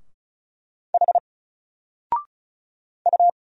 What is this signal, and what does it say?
Morse code, heard as a keyed tone, about 36 wpm, 700 Hz. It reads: H U
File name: Single-Letter-Number-35wpm